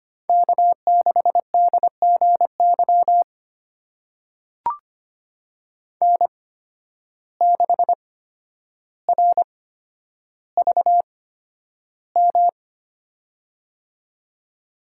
Morse code, heard as a keyed tone, about 25 wpm, spaced as for 11 wpm, 700 Hz. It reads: K6DGY N6RVM
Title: USA Call Signs - 5 Characters - 8x Character Spacing - 25wpm